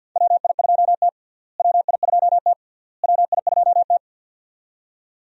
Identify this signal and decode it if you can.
Morse code, heard as a keyed tone, about 54 wpm, 700 Hz. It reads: WI2T WI2T WI2T
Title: USA Call Signs - 4 Characters - Repeated 3x - 50wpm